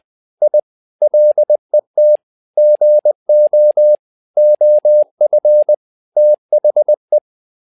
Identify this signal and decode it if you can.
Morse code, heard as a keyed tone, about 20 wpm, 600 Hz. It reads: I LET GO OF THE
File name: tower trouble 20wpm